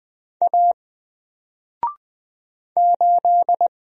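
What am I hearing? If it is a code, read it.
Morse code, heard as a keyed tone, about 20 wpm, 700 Hz. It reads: A 8
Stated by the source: Single letter and number Morse drill at 20 wpm, from N4LCD.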